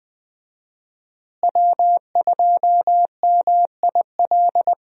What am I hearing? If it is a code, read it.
Morse code, heard as a keyed tone, about 20 wpm, 700 Hz. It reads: W2MIL
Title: USA Call Signs - 5 Characters - Repeated 2x - 20wpm